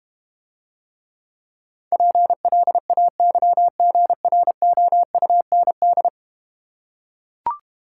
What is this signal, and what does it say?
Morse code, heard as a keyed tone, about 32 wpm, 700 Hz. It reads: PLAYGROUND